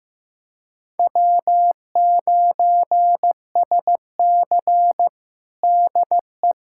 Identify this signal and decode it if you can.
Morse code, heard as a keyed tone, about 15 wpm, 700 Hz. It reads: W9SC DE
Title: USA Callsign DE Callsign - 4 Characters - 15wpm